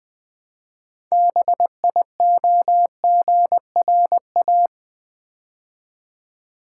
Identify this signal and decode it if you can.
Morse code, heard as a keyed tone, about 20 wpm, 700 Hz. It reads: BIOGRA
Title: Build A Word - 3 Letters At A Time - 9-Letter Words - 20wpm